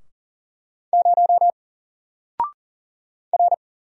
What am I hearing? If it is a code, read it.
Morse code, heard as a keyed tone, about 40 wpm, 700 Hz. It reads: 0 R
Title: Single Letter-Number 40wpm